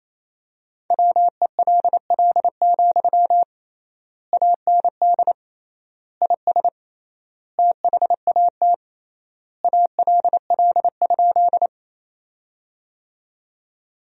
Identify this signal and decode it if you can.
Morse code, heard as a keyed tone, about 28 wpm, 700 Hz. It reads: WELL, AND IS THAT ALL?
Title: 3x Word-Spacing - Sentences from Top 200 Words - 28wpm